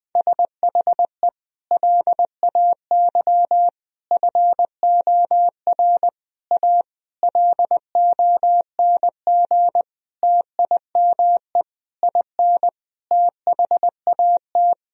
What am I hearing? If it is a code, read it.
Morse code, heard as a keyed tone, about 20 wpm, 700 Hz. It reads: SHE LAY FOR A LONG TIME IN THAT
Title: Sentences from Top 1000 Words - Review Part 2 - 20wpm